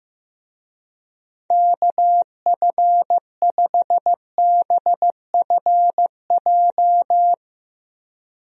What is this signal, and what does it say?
Morse code, heard as a keyed tone, about 15 wpm, 700 Hz. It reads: KF5BFJ
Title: USA Call Signs - 6 Characters - Repeated 3x - 15wpm